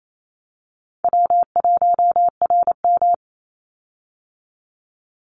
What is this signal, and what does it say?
Morse code, heard as a keyed tone, about 28 wpm, 700 Hz. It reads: W1RM
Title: Call Signs - 28wpm